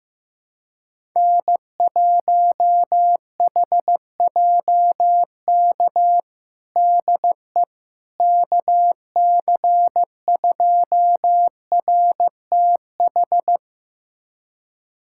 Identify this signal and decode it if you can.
Morse code, heard as a keyed tone, about 15 wpm, 700 Hz. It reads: N1HJK DE KC2RTH
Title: USA Callsign DE Callsign 15wpm